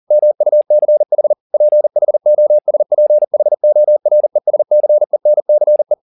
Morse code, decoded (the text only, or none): MUCH PHOSPHORESCENCE